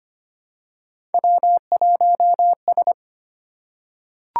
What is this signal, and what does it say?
Morse code, heard as a keyed tone, about 25 wpm, 700 Hz. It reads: W1S
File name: Three Letter-Number Combinations - 25wpm